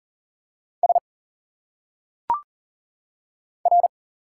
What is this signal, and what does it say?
Morse code, heard as a keyed tone, about 38 wpm, 700 Hz. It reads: S R
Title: Single Letters - Phonetic Pronunciation 40wpm